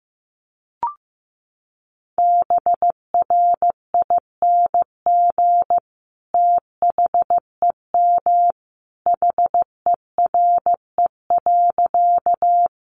Morse code, heard as a keyed tone, about 15 wpm, 700 Hz. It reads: BRING THEM HERE.
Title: Sentences from Top 500 Words 15wpm